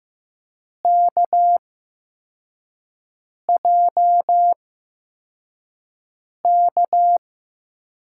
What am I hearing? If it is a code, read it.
Morse code, heard as a keyed tone, about 15 wpm, spaced as for 4 wpm, 700 Hz. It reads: KJK
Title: USA Call Signs - 6 Characters - 8x Character Spacing - 15wpm